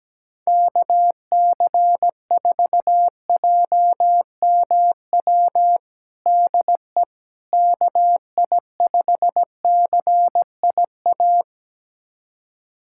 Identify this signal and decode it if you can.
Morse code, heard as a keyed tone, about 17 wpm, 700 Hz. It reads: KC4JMW DE KI5CIA
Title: USA Callsign DE Callsign 17wpm